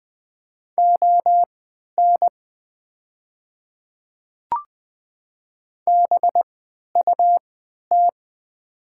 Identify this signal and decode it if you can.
Morse code, heard as a keyed tone, about 20 wpm, spaced as for 11 wpm, 700 Hz. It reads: ON BUT
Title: 3x Character Spacing - Top 300 Words - 20wpm